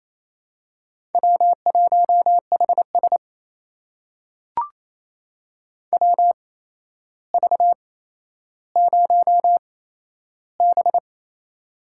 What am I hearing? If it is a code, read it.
Morse code, heard as a keyed tone, about 27 wpm, spaced as for 12 wpm, 700 Hz. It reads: W1HS WV0B
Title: USA Call Signs - 4 Characters - 8x Character Spacing - 28wpm